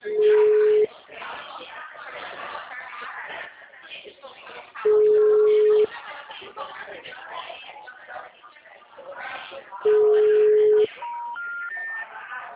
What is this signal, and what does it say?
Ringback tone (Europe, 425 Hz), with speech.